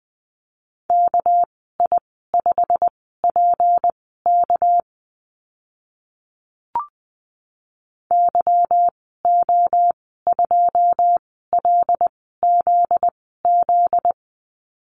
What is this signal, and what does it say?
Morse code, heard as a keyed tone, about 20 wpm, spaced as for 15 wpm, 700 Hz. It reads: KI5PK YO2LZZ
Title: 245 Lesson 39g - Letter z - Call Signs 20wpm